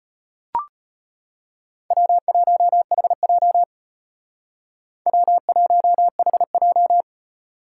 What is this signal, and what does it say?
Morse code, heard as a keyed tone, about 31 wpm, 700 Hz. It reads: W1HJ W1HJ
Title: Speed Racing - USA 1x2 Call Signs 25wpm